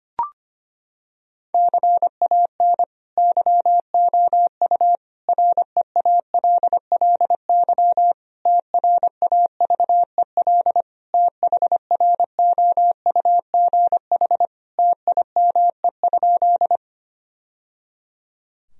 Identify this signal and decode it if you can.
Morse code, heard as a keyed tone, about 25 wpm, 700 Hz. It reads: CAN YOU REALLY TRAVEL THROUGH TIME?
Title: Sentences from Top 1000 Words - Review Part 2 - 25wpm